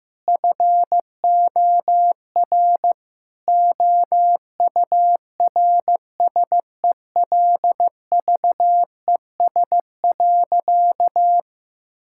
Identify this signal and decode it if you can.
Morse code, heard as a keyed tone, about 15 wpm, 700 Hz. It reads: FOR OURSELVES.
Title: Sentences from Top 1600 Words 15wpm